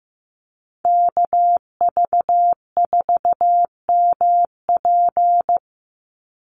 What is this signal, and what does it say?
Morse code, heard as a keyed tone, about 15 wpm, 700 Hz. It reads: KV4MP
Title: USA Call Signs 15wpm